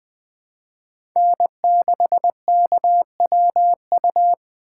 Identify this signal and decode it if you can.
Morse code, heard as a keyed tone, about 20 wpm, 700 Hz. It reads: N6KWU
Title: USA Call Signs - 5 Characters - 3x Character Spacing - 20wpm